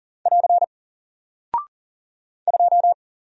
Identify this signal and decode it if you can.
Morse code, heard as a keyed tone, about 40 wpm, 700 Hz. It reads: <AR> 2
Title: Single Character (Letter:Number:Punctuation:Prosign) 40wpm